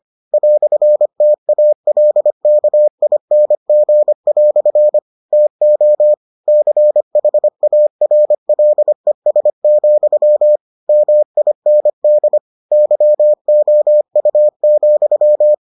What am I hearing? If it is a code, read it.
Morse code, heard as a keyed tone, about 25 wpm, 600 Hz. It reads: "TALKING" TO CHARLES, MIND YOU,